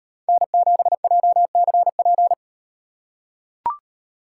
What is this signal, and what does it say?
Morse code, heard as a keyed tone, about 37 wpm, 700 Hz. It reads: N7JCP